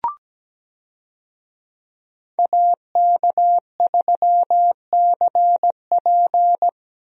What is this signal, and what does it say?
Morse code, heard as a keyed tone, about 17 wpm, 700 Hz. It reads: AK3CP
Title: USA Call Signs - 5 Characters - Repeated 2x - 17wpm